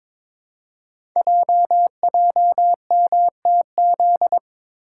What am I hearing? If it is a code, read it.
Morse code, heard as a keyed tone, about 22 wpm, 700 Hz. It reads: JJMTZ
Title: Code Groups - 5 Letters - 22wpm